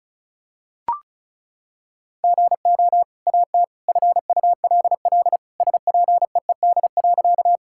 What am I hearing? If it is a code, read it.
Morse code, heard as a keyed tone, about 34 wpm, 700 Hz. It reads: GO AT FULL SPEED.
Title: Sentences from Top 1000 Words - Review Part 5 - 35wpm